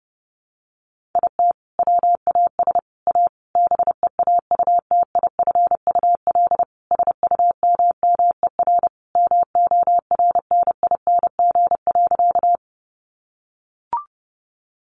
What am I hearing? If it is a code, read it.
Morse code, heard as a keyed tone, about 30 wpm, 700 Hz. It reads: IT WAS A BEAUTIFUL SUMMER MORNING.
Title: Sentences from Top 1100 Words 30wpm